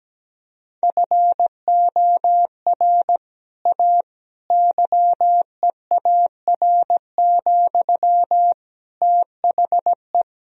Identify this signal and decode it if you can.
Morse code, heard as a keyed tone, about 17 wpm, 700 Hz. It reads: FOR A YEAR, THE
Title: Sentences from Top 1000 Words - Review Part 1 - 17wpm